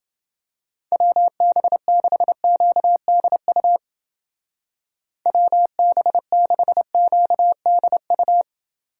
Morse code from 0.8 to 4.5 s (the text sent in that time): WB6QDU